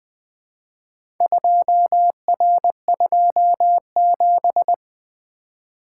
Morse code, heard as a keyed tone, about 20 wpm, 700 Hz. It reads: 2R27
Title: Four Letter-Number Combinations - 20wpm